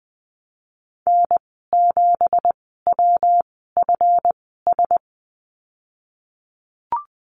Morse code, heard as a keyed tone, about 19 wpm, spaced as for 14 wpm, 700 Hz. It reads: N7WFS
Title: 200 Lesson 32g - Letter k - Call Signs 20wpm